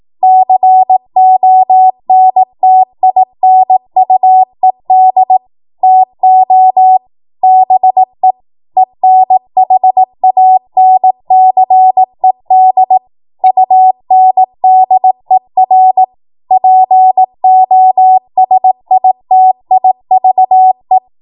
Morse code, heard as a keyed tone, about 18 wpm, 750 Hz. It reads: CONTINUED TO BE ENHANCED UNDER POSITIVE
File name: Morse / 18WPM